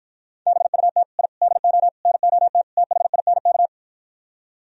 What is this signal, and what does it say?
Morse code, heard as a keyed tone, about 53 wpm, 700 Hz. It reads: BUT I DO NOT THINK